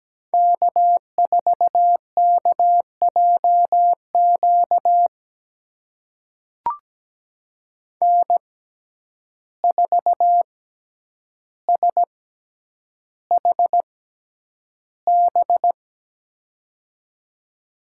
Morse code, heard as a keyed tone, about 17 wpm, spaced as for 9 wpm, 700 Hz. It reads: K4KJQ N4SHB
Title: USA Call Signs - 5 Characters - 6x Character Spacing - 17wpm